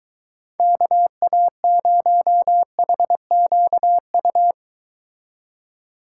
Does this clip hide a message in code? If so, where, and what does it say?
Yes: Morse code from 0.6 to 4.5 s, heard as a keyed tone, about 24 wpm, 700 Hz. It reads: KA0HQU